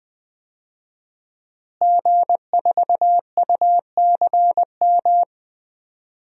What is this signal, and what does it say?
Morse code, heard as a keyed tone, about 20 wpm, 700 Hz. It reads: G4UCM
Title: DXCC-100 Call Signs - 5 Characters - Repeated 2x - 20wpm